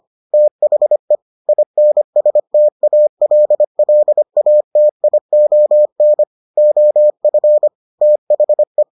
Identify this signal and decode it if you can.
Morse code, heard as a keyed tone, about 25 wpm, 600 Hz. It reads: THE INSTALLATION OF THE